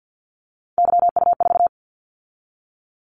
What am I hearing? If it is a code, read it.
Morse code, heard as a keyed tone, about 50 wpm, 700 Hz. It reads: YU4
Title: Callsign Prefixes for Top 25 DXCC - 50wpm